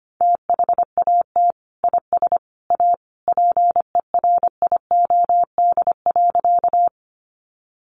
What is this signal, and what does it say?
Morse code, heard as a keyed tone, about 25 wpm, 700 Hz. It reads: THAT IS A PERIOD.
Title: Sentences from Top 800 Words 25wpm